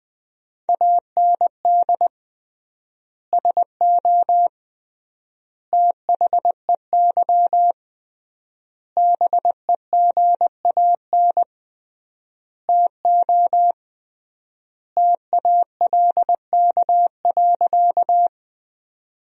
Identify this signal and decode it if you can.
Morse code, heard as a keyed tone, about 21 wpm, 700 Hz. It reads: AND SO THEY BEGAN TO TALK.
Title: 3x Word-Spacing - Sentences from Top 400 Words - 20wpm